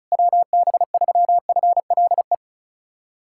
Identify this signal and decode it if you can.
Morse code, heard as a keyed tone, about 34 wpm, 700 Hz. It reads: WB3FLE
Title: USA Callsign DE Callsign 35wpm